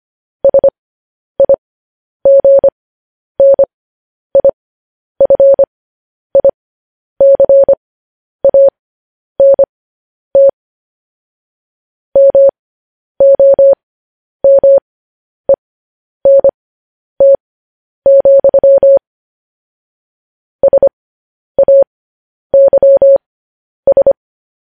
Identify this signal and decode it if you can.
Morse code, heard as a keyed tone, about 25 wpm, spaced as for 10 wpm, 550 Hz. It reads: SIGNIFICANT MOMENT, SAYS
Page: News Headlines in Morse Code at 10 WPM